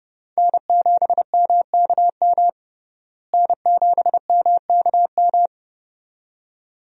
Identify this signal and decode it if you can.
Morse code, heard as a keyed tone, about 30 wpm, 700 Hz. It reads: N7MKM N7MKM
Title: USA Call Signs - 5 Characters - Repeated 2x - 30wpm